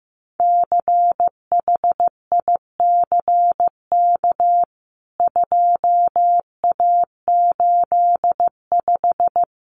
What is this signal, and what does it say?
Morse code, heard as a keyed tone, about 15 wpm, 700 Hz. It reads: CHICK 2A85
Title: CWT Exchange 15wpm